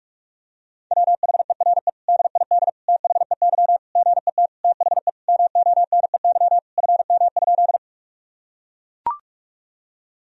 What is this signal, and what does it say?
Morse code, heard as a keyed tone, about 41 wpm, 700 Hz. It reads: WHERE DID THEY GET THE MONEY FM?
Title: Sentences from Top 400 Words and Abbreviations 45wpm